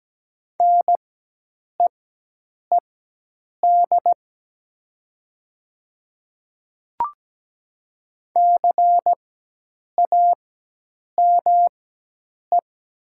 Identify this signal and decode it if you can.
Morse code, heard as a keyed tone, about 17 wpm, spaced as for 8 wpm, 690 Hz. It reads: NEED CAME
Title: 4x Character Spacing - Top 300 Words - 17wpm